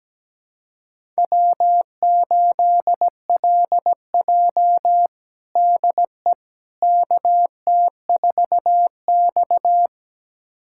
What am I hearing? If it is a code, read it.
Morse code, heard as a keyed tone, about 17 wpm, 700 Hz. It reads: W8LJ DE KT4X
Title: USA Callsign DE Callsign - 4 Characters - 17wpm